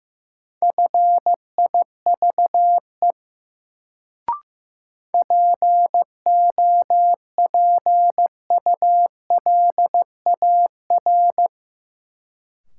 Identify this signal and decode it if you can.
Morse code, heard as a keyed tone, about 15 wpm, 700 Hz. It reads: FIVE POPULAR
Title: Sets of 5 Words - New Word Drill - 15wpm